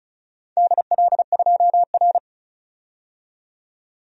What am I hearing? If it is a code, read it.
Morse code, heard as a keyed tone, about 36 wpm, 700 Hz. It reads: DL2R
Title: DXCC-100 Call Signs - 4 Characters - 35wpm